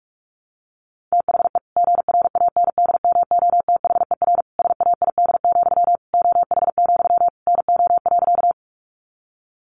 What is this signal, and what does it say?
Morse code, heard as a keyed tone, about 44 wpm, 700 Hz. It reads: THE GRANDMOTHER SAID, OH, NO.
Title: Sentences from Top 1200 Words 45wpm